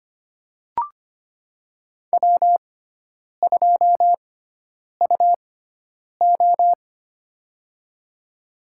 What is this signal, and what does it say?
Morse code, heard as a keyed tone, about 25 wpm, spaced as for 9 wpm, 700 Hz. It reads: W2UO
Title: USA Call Signs - 4 Characters - 6x Character Spacing - 25wpm